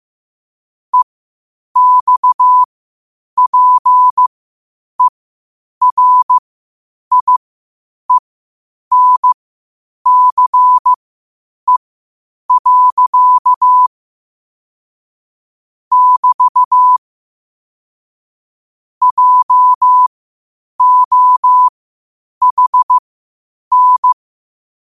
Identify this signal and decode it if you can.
Morse code, heard as a keyed tone, about 15 wpm, spaced as for 8 wpm, 1000 Hz. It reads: EXPERIENCE. <BT> JOHN
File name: qotdfiles / 8wpm